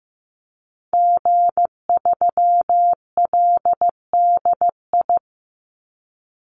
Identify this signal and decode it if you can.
Morse code, heard as a keyed tone, about 15 wpm, 700 Hz. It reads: G3LDI
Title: Call Signs Repeated 2x - 15wpm